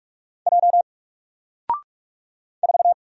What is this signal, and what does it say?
Morse code, heard as a keyed tone, about 44 wpm, 700 Hz. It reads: J 4
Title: Single Letter-Number Alt 45wpm